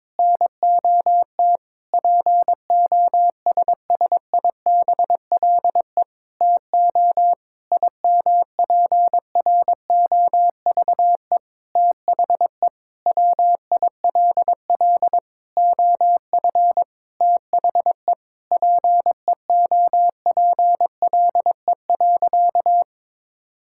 Morse code, heard as a keyed tone, about 22 wpm, 700 Hz. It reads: NOT POSSIBLE TO IMPROVE THE WILL OF THE PEOPLE.
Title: Sentences from Top 1000 Words - Review Part 5 - 22wpm